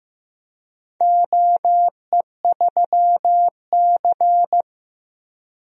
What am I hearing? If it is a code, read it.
Morse code, heard as a keyed tone, about 15 wpm, 700 Hz. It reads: OE3C
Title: DXCC-100 Call Signs - 4 Characters - Repeated 2x - 15wpm